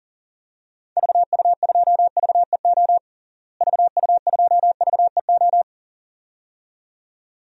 Morse code, heard as a keyed tone, about 37 wpm, 700 Hz. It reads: VU2VEO VU2VEO
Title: DXCC-100 Call Signs - 6 Characters - Repeated 2x - 40wpm